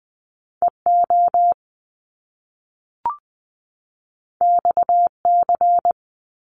Morse code, heard as a keyed tone, about 20 wpm, 700 Hz. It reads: EO XC
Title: Top Two Letter Combinations in Words 20wpm